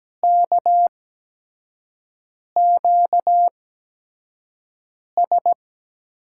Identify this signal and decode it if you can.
Morse code, heard as a keyed tone, about 17 wpm, spaced as for 5 wpm, 700 Hz. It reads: KQS
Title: USA Call Signs - 8x Character Spacing - 17wpm